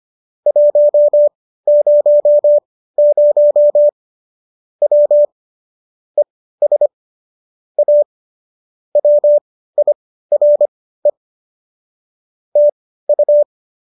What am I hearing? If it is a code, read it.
Morse code, heard as a keyed tone, about 25 wpm, spaced as for 15 wpm, 590 Hz. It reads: 100 W ES A WIRE TU